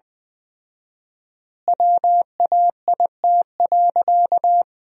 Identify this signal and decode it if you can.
Morse code, heard as a keyed tone, about 20 wpm, 700 Hz. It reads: WAIT.
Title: 4x Word-Spacing - Sentences from Top 500 Words - 20wpm